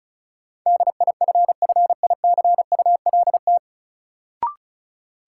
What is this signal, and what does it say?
Morse code, heard as a keyed tone, about 36 wpm, 700 Hz. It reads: DIFFICULT
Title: Sets of 4 Words - New Word Drill - 35wpm